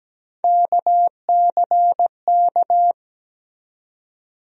Morse code, heard as a keyed tone, about 17 wpm, 700 Hz. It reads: KCK
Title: USA Call Sign Postfixes - 3 Letters 17wpm